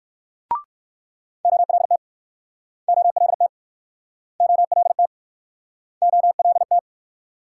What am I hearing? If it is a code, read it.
Morse code, heard as a keyed tone, about 63 wpm, 700 Hz. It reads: OLT OLT OLT OLT